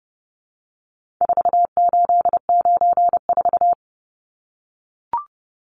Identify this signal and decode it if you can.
Morse code, heard as a keyed tone, about 30 wpm, 700 Hz. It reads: 4894
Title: Numbers - 4-Digit 30wpm